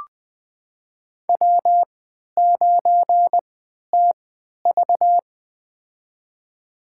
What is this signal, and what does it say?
Morse code, heard as a keyed tone, about 20 wpm, spaced as for 11 wpm, 700 Hz. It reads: W9TV
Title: USA Call Signs - 4 Characters - 3x Character Spacing - 20wpm